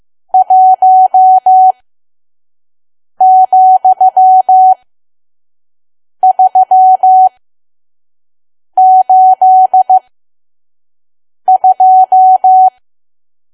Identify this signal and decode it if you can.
Morse code, heard as a keyed tone, about 15 wpm, spaced as for 5 wpm, 750 Hz. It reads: 1,382
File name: Archive / 5 WPM files / PM